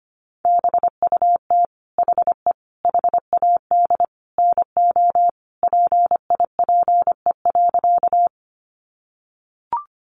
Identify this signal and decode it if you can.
Morse code, heard as a keyed tone, about 25 wpm, 700 Hz. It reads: BUT HE HAD NO PIPE.
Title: Sentences from Top 1500 Words 25wpm